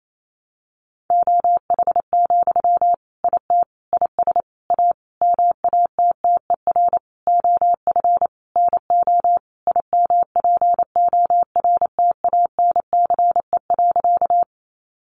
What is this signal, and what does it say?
Morse code, heard as a keyed tone, about 28 wpm, 700 Hz. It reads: OH, IT IS A MATTER OF NO IMPORTANCE.
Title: Sentences from Top 1200 Words 28wpm